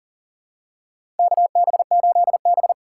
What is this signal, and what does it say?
Morse code, heard as a keyed tone, about 40 wpm, 700 Hz. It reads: KB8B
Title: USA Call Signs - 3x Character Spacing - 40wpm